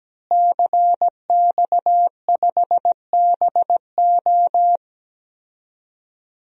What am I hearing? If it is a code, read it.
Morse code, heard as a keyed tone, about 17 wpm, 700 Hz. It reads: CX5BO
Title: DXCC-100 Call Signs - 5 Characters - Repeated 3x - 17wpm